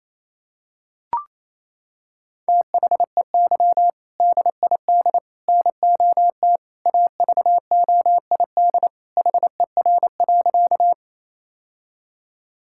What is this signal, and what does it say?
Morse code, heard as a keyed tone, about 27 wpm, 700 Hz. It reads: THEY DID NOT AVOID HER.
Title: Sentences from Top 1000 Words - Review Part 6 - 28wpm